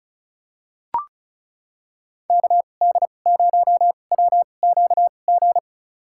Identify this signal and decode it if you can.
Morse code, heard as a keyed tone, about 36 wpm, spaced as for 25 wpm, 700 Hz. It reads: KD0WQG
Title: USA Call Signs - 6 Characters - 2x Character Spacing - 35wpm